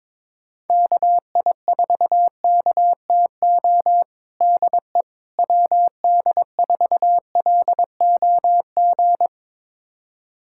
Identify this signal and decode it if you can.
Morse code, heard as a keyed tone, about 22 wpm, 700 Hz. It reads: KI4KTO DE WD4LOG
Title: USA Callsign DE Callsign - 6 Characters - 22wpm